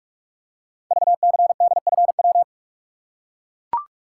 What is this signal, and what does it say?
Morse code, heard as a keyed tone, about 44 wpm, 700 Hz. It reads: VCDFW
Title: Code Groups - 5 Letters - 45wpm